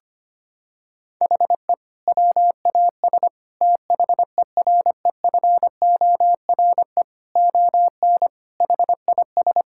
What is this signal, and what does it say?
Morse code, heard as a keyed tone, about 25 wpm, 700 Hz. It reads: HE WAS THEREFORE ON HIS
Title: Sentences from Top 1000 Words - Review Part 5 - 25wpm